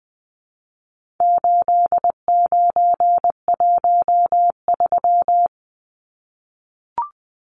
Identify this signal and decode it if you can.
Morse code, heard as a keyed tone, about 20 wpm, 700 Hz. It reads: 8913
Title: Numbers - 4-Digit 20wpm